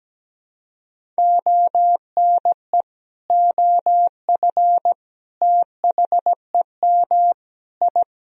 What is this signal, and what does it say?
Morse code, heard as a keyed tone, about 17 wpm, 700 Hz. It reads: ONE OF THEM I
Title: Sentences from Top 1000 Words - Review Part 1 - 17wpm